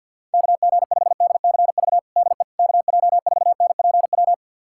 Morse code, heard as a keyed tone, about 51 wpm, 700 Hz. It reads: KG5DCV DE KJ4NPW
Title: USA Callsign DE Callsign - 6 Characters - 50wpm